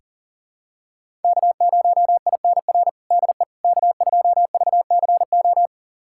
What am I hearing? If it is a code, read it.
Morse code, heard as a keyed tone, about 40 wpm, 700 Hz. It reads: K0INR DE K2VCO
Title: USA Callsign DE Callsign - 5 Characters - 40wpm